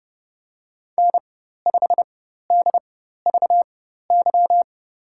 Morse code, heard as a keyed tone, about 30 wpm, spaced as for 14 wpm, 700 Hz. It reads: N5DVY
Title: USA Call Signs - 4x Character Spacing - 30wpm